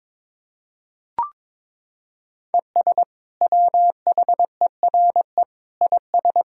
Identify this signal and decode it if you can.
Morse code, heard as a keyed tone, about 21 wpm, 700 Hz. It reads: ES WHERE IS
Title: Sentences from Top 200 Words and Abbreviations 22wpm